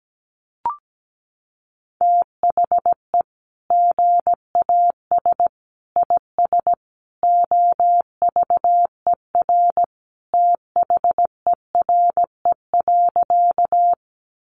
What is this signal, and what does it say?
Morse code, heard as a keyed tone, about 17 wpm, 700 Hz. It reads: THE GAS IS OVER THERE.
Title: Sentences from Top 1100 Words 17wpm